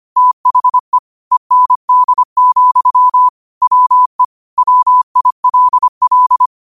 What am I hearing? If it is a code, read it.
Morse code, heard as a keyed tone, about 25 wpm, 1000 Hz. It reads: THE END, WE WILL